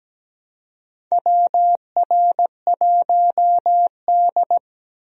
Morse code, heard as a keyed tone, about 17 wpm, 700 Hz. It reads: WR1D
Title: USA Call Signs - 4 Characters - 17wpm